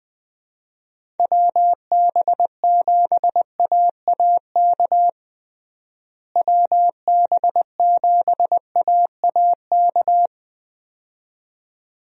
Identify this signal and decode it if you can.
Morse code, heard as a keyed tone, about 21 wpm, 700 Hz. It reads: WB7AAK WB7AAK
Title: USA Call Signs - 6 Characters - Repeated 2x - 20wpm